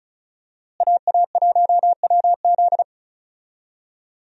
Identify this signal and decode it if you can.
Morse code, heard as a keyed tone, about 34 wpm, 700 Hz. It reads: AA1WZ
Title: USA Call Signs - 5 Characters - Repeated 2x - 35wpm